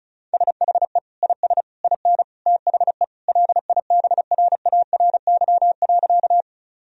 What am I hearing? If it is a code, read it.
Morse code, heard as a keyed tone, about 36 wpm, 700 Hz. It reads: SHE IS IN THE LIBRARY.
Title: Sentences from Top 1000 Words - Review Part 4 - 35wpm